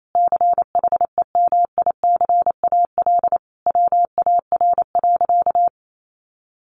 Morse code, heard as a keyed tone, about 28 wpm, 700 Hz. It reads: CHEMICAL WAR.
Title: Sentences from Top 1200 Words 28wpm